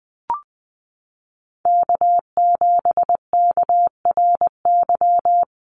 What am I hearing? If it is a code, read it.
Morse code, heard as a keyed tone, about 19 wpm, 700 Hz. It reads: K7KRY
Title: USA Call Signs Repeated 2x 20wpm